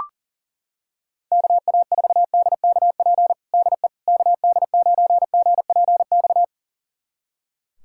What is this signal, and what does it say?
Morse code, heard as a keyed tone, about 40 wpm, 700 Hz. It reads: KA4DKP DE KD9GPX
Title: USA Callsign DE Callsign 40wpm